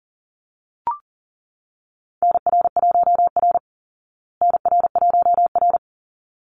Morse code, heard as a keyed tone, about 40 wpm, 700 Hz. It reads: NR1R NR1R
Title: Call Signs Repeated 3x - 40wpm